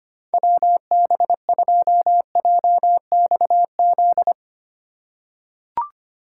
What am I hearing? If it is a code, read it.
Morse code, heard as a keyed tone, about 25 wpm, 700 Hz. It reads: WB2JXZ
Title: USA Call Signs - 6x Character Spacing - 25wpm